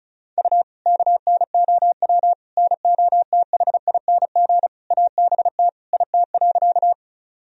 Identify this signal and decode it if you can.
Morse code, heard as a keyed tone, about 36 wpm, 700 Hz. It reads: U KNOW NOTHING ABT IT.
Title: Sentences from Top 500 Words and Abbreviations 35wpm